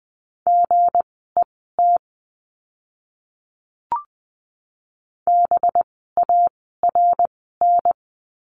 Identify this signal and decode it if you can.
Morse code, heard as a keyed tone, about 20 wpm, spaced as for 14 wpm, 700 Hz. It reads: GET BARN
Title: 185 Lesson 30f - Letter b - Words 20wpm